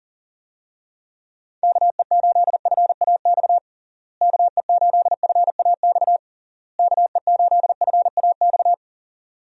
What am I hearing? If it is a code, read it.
Morse code, heard as a keyed tone, about 40 wpm, 700 Hz. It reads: KE8FAX KE8FAX KE8FAX
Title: USA Call Signs - 6 Characters - Repeated 3x - 40wpm